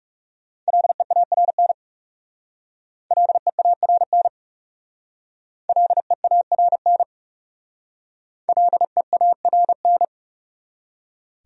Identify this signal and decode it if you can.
Morse code, heard as a keyed tone, about 36 wpm, 700 Hz. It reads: LEARN LEARN LEARN LEARN
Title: Speed-Racing Top 500 Words 30wpm